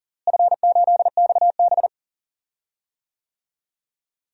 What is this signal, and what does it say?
Morse code, heard as a keyed tone, about 40 wpm, 700 Hz. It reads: F8XB
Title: DXCC-100 Call Signs - 4 Characters - Repeated 3x - 40wpm